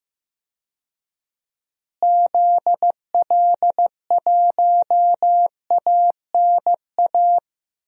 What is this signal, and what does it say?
Morse code, heard as a keyed tone, about 15 wpm, 700 Hz. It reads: ZL1ANA
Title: DXCC-100 Call Signs - 6 Characters - Repeated 3x - 15wpm